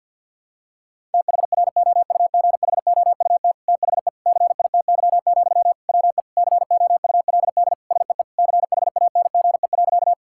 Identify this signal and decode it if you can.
Morse code, heard as a keyed tone, about 52 wpm, 700 Hz. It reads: THROUGHOUT THE CITY, WE COULD SEE CHANGE.
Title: Sentences from Top 1000 Words - Review Part 5 - 50wpm